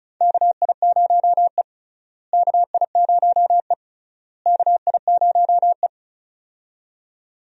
Morse code, heard as a keyed tone, about 36 wpm, 700 Hz. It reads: KI0E KI0E KI0E
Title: USA Call Signs - 4 Characters - Repeated 3x - 35wpm